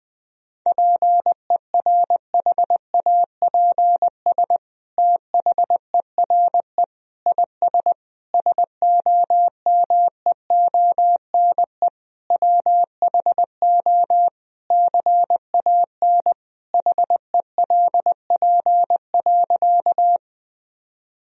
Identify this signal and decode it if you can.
Morse code, heard as a keyed tone, about 21 wpm, 700 Hz. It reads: PERHAPS THERE IS SOMEONE WHO CAN HELP.
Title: Sentences from Top 1000 Words - Review Part 4 - 20wpm